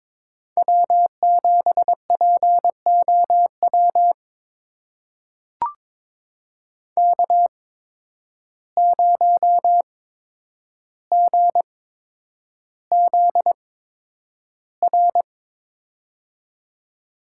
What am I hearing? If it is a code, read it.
Morse code, heard as a keyed tone, about 22 wpm, spaced as for 9 wpm, 700 Hz. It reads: W7POW K0GZR
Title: USA Call Signs - 5 Characters - 8x Character Spacing - 22wpm